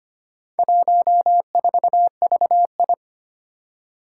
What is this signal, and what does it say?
Morse code, heard as a keyed tone, about 25 wpm, 700 Hz. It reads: 14VI
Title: Four Challenging Letter-Number Combinations 25wpm